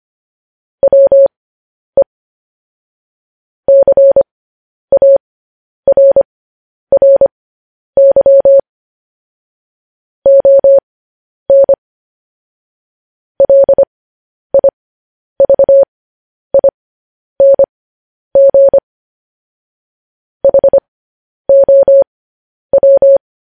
Morse code, heard as a keyed tone, about 25 wpm, spaced as for 10 wpm, 550 Hz. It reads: WE CARRY ON LIVING HOW